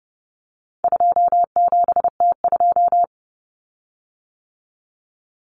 Morse code, heard as a keyed tone, about 29 wpm, 700 Hz. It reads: 27T2
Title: Common Cut Numbers - 4-Digit 30wpm